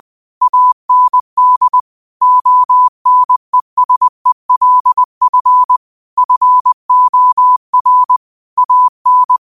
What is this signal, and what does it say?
Morse code, heard as a keyed tone, about 20 wpm, 1000 Hz. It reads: AND ONESELF FOR AN
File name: qotdfiles / 20wpm